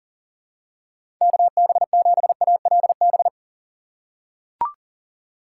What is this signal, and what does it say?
Morse code, heard as a keyed tone, about 39 wpm, 700 Hz. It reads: KB7ALB
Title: USA Call Signs - 6x Character Spacing - 40wpm